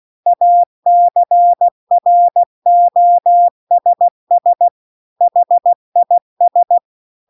Morse code, heard as a keyed tone, about 16 wpm, 700 Hz. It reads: ACROSS HIS